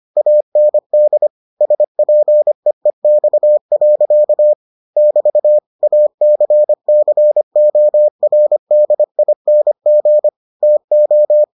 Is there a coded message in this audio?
Yes: Morse code from 0.2 to 11.4 s, heard as a keyed tone, about 25 wpm, 600 Hz. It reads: AND SPEEX. <BT> ACCORDING TO